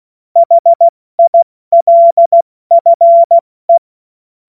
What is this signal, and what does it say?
Morse code, heard as a keyed tone, about 16 wpm, 680 Hz. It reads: HILFE
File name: Morsekurs / Witze / Witz 17. August 2021, 70 F80